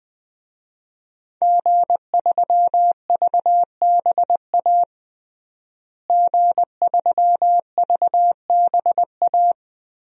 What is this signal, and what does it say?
Morse code, heard as a keyed tone, about 20 wpm, 700 Hz. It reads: G3VBA G3VBA
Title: DXCC-100 Call Signs - 5 Characters - Repeated 3x - 20wpm